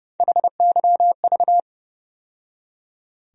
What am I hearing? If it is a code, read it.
Morse code, heard as a keyed tone, about 30 wpm, 700 Hz. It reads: HYV
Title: USA Call Sign Postfixes - 3 Letters 30wpm